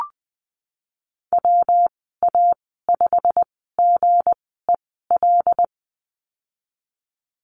Morse code, heard as a keyed tone, about 20 wpm, spaced as for 15 wpm, 700 Hz. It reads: WA5GEL
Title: 173 Lesson 28g - Number 9 - Call Signs 20wpm